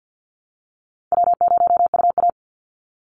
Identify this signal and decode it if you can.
Morse code, heard as a keyed tone, about 50 wpm, 700 Hz. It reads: W0UA
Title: Easy Call Signs Repeated 3x - 50wpm